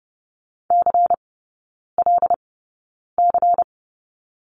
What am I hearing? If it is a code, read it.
Morse code, heard as a keyed tone, about 30 wpm, spaced as for 9 wpm, 700 Hz. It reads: CLC
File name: Lesson 16e - Letter c - Three Characters 30wpm